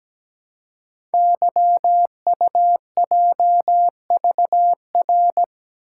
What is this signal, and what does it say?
Morse code, heard as a keyed tone, about 17 wpm, 700 Hz. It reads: YUJVR
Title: Code Groups - 5 Letters - 17wpm